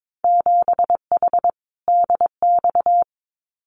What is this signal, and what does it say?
Morse code, heard as a keyed tone, about 22 wpm, 700 Hz. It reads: 7H DX
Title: Winter Field Day Exchanges Encore 22wpm